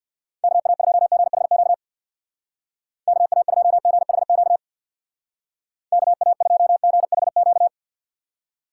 Morse code, heard as a keyed tone, about 60 wpm, 690 Hz. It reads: KA2GHQ KA2GHQ KA2GHQ
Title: Speed Racing - USA 2x3 Call Signs 50wpm